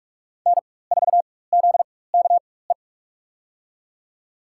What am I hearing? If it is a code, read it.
Morse code, heard as a keyed tone, about 46 wpm, spaced as for 21 wpm, 700 Hz. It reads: N4ZKE
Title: USA Call Signs - 5 Characters - 4x Character Spacing - 45wpm